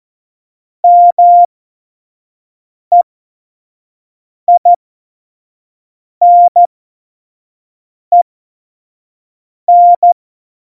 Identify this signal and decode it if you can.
Morse code, heard as a keyed tone, about 14 wpm, spaced as for 5 wpm, 700 Hz. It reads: MEINEN
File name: Morsekurs / Witze / Witz 27. Februar 2018, 25 F70